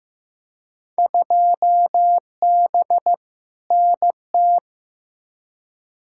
Morse code, heard as a keyed tone, about 15 wpm, 700 Hz. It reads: 2B NT
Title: Field Day Exchanges Encore - 15wpm